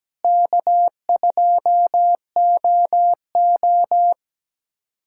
Morse code, heard as a keyed tone, about 17 wpm, 700 Hz. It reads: K2OO
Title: USA Callsign DE Callsign - 4 Characters - 17wpm